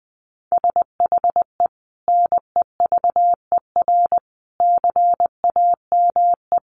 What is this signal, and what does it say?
Morse code, heard as a keyed tone, about 20 wpm, 700 Hz. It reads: SHE NEVER CAME
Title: Sentences from Top 300 Words 20wpm